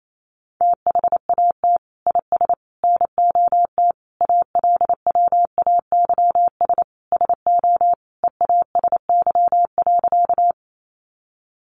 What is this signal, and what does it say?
Morse code, heard as a keyed tone, about 28 wpm, 700 Hz. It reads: THAT IS NOT ALWAYS SO EASY.
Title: Sentences from Top 700 Words 28wpm